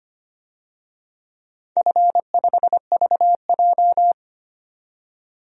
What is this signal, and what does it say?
Morse code, heard as a keyed tone, about 25 wpm, 700 Hz. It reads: F5VJ
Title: DXCC-100 Call Signs - 4 Characters - 25wpm